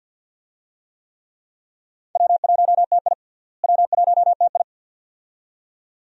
Morse code, heard as a keyed tone, about 48 wpm, 700 Hz. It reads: W1TI W1TI
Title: USA Call Signs - 4 Characters - Repeated 2x - 50wpm